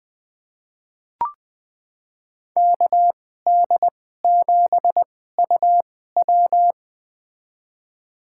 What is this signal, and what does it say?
Morse code, heard as a keyed tone, about 21 wpm, spaced as for 15 wpm, 700 Hz. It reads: KD7UW
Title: USA Call Signs - 5 Characters - 2x Character Spacing - 20wpm